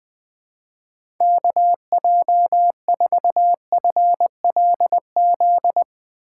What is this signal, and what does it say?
Morse code, heard as a keyed tone, about 20 wpm, 700 Hz. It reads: KJ4FLZ
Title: USA Call Signs - 4x Character Spacing - 20wpm